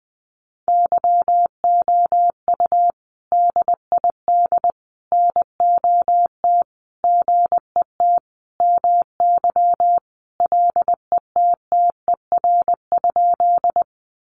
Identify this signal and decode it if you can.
Morse code, heard as a keyed tone, about 20 wpm, 700 Hz. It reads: YOU DID NOT GET MY LETTER?
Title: Sentences from Top 600 Words 20wpm